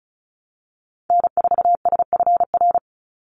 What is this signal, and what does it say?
Morse code, heard as a keyed tone, about 34 wpm, 700 Hz. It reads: N4SFR
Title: NA SOTA Activator Callsigns 35wpm